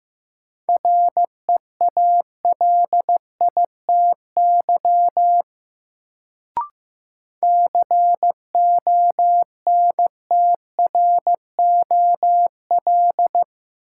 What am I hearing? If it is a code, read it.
Morse code, heard as a keyed tone, about 16 wpm, 700 Hz. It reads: REALITY CONTROL
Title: Sentences from Top 1200 Words - New Word Drill Alt - 15wpm